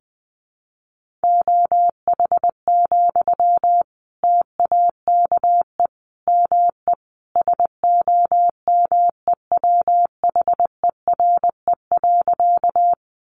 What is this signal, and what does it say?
Morse code, heard as a keyed tone, about 20 wpm, 700 Hz. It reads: OH, TAKE ME SOMEWHERE.
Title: Sentences from Top 1400 Words 20wpm